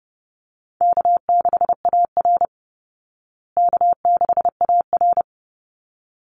Morse code, heard as a keyed tone, about 30 wpm, 700 Hz. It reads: K6AR K6AR
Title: Easy Call Signs Repeated 2x - 30wpm